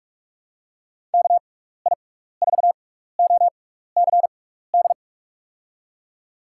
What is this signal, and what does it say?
Morse code, heard as a keyed tone, about 44 wpm, spaced as for 15 wpm, 700 Hz. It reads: KI4OCD
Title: USA Call Signs - 6 Characters - 6x Character Spacing - 45wpm